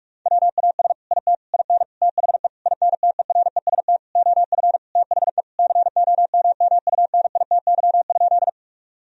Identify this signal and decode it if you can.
Morse code, heard as a keyed tone, about 46 wpm, 700 Hz. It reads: WAS IT IN THE INTEREST OF THE COMMUNITY?
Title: Sentences from Top 1000 Words - Review Part 4 - 45wpm